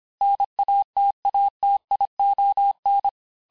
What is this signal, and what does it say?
Morse code, heard as a keyed tone, about 26 wpm, 800 Hz. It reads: NATATION
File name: cours-du-lundi files / cours-1584-teste-clair-vitesse-19mpm-puis-24mpm